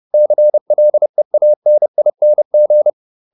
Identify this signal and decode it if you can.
Morse code, heard as a keyed tone, about 30 wpm, 600 Hz. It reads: CLEANING
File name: ING8 30